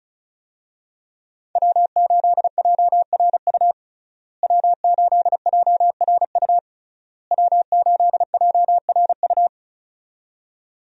Morse code, heard as a keyed tone, about 36 wpm, 700 Hz. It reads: W8JRU W8JRU W8JRU
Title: USA Call Signs - 5 Characters - Repeated 3x - 35wpm